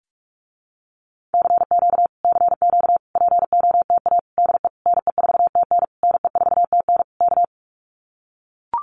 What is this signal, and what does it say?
Morse code, heard as a keyed tone, about 44 wpm, 700 Hz. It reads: CQ CQ POTA DE NE4TN NE4TN K
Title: Common QSO Phrases Encore 45wpm